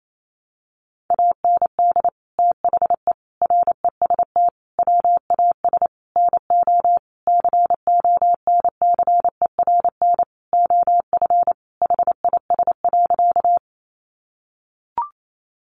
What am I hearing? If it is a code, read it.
Morse code, heard as a keyed tone, about 28 wpm, 700 Hz. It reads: AND THE REST WAS NO CONCERN OF HIS.
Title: Sentences from Top 1000 Words 28wpm